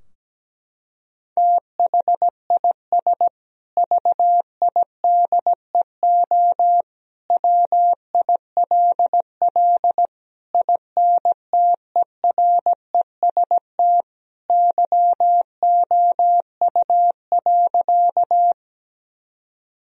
Morse code, heard as a keyed tone, about 17 wpm, 700 Hz. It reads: THIS VIDEO WILL INTEREST YOU.
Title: Sentences from Top 1000 Words - Review Part 4 - 17wpm